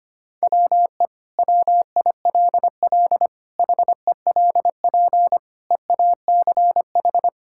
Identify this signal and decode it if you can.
Morse code, heard as a keyed tone, about 25 wpm, 700 Hz. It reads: WE WILL HELP EACH
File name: practice / 25wpm / Sentences with 2-Word Phrases - 25wpm